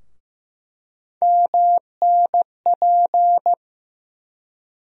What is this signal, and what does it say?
Morse code, heard as a keyed tone, about 15 wpm, 700 Hz. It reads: MNP
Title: USA Call Sign Postfixes - 3 Letters 15wpm